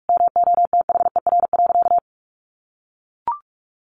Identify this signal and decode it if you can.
Morse code, heard as a keyed tone, about 44 wpm, 700 Hz. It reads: MOTHER.
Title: Sentences from Top 1300 Words 45wpm